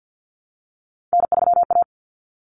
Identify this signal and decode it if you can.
Morse code, heard as a keyed tone, about 50 wpm, 700 Hz. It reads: N3A
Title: Common Cut Numbers - 3-Digit 50wpm